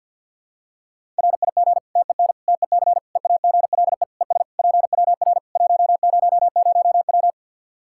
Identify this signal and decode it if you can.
Morse code, heard as a keyed tone, about 51 wpm, 700 Hz. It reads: RIG TEN TEC EAGLE ES PWR 100W
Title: Practice QSO - Exchange 2 - Info - 50wpm